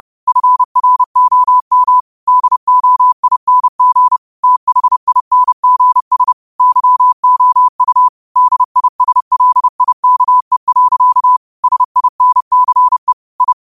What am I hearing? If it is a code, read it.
Morse code, heard as a keyed tone, about 30 wpm, 1000 Hz. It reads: FROM DOING THINGS YOU DISLIKE. SINCE I
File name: qotdfiles / 30wpm